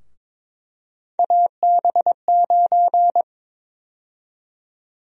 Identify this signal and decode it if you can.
Morse code, heard as a keyed tone, about 22 wpm, 700 Hz. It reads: AB9
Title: USA Call Sign Prefixes - Repeated 2x - 22wpm